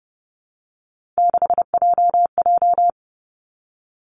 Morse code, heard as a keyed tone, about 30 wpm, 700 Hz. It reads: 6JJ
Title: Three Challenging Letter-Number Combinations 30wpm